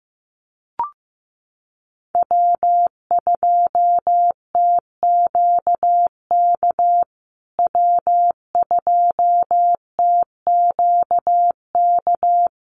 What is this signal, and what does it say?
Morse code, heard as a keyed tone, about 15 wpm, 700 Hz. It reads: W2TQK W2TQK
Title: USA Call Signs Repeated 3x 15wpm